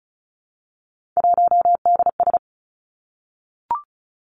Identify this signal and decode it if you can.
Morse code, heard as a keyed tone, about 36 wpm, 700 Hz. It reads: 1DS
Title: Three Challenging Letter-Number Combinations 35wpm